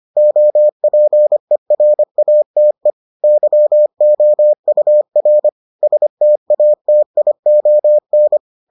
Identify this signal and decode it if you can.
Morse code, heard as a keyed tone, about 25 wpm, 600 Hz. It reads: OPERATE YOUR STATION